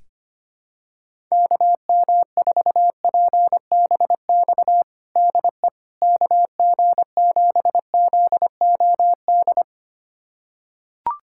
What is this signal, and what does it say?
Morse code, heard as a keyed tone, about 25 wpm, 700 Hz. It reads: KM4PBX DE KG7ZOD
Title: USA Callsign DE Callsign - 6 Characters - 25wpm